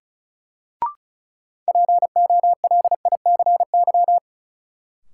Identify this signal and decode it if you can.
Morse code, heard as a keyed tone, about 34 wpm, 700 Hz. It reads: POLICY
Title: Sentences from Top 1000 Words - Review Part 5 - New Word Drill - 35wpm